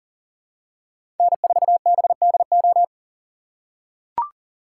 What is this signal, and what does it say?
Morse code, heard as a keyed tone, about 42 wpm, 700 Hz. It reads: N4BDO
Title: USA Call Signs - 5 Characters - 6x Character Spacing - 40wpm